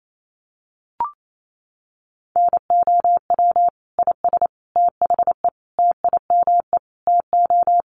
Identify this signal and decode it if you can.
Morse code, heard as a keyed tone, about 28 wpm, 700 Hz. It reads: NOW IS THE TIME TO
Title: Sets of 5 Words Encore2 28wpm